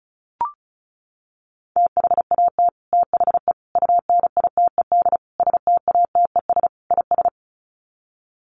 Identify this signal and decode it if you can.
Morse code, heard as a keyed tone, about 34 wpm, 700 Hz. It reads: THAT THE UNITED STATES IS
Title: Sets of 5 Words Encore 35wpm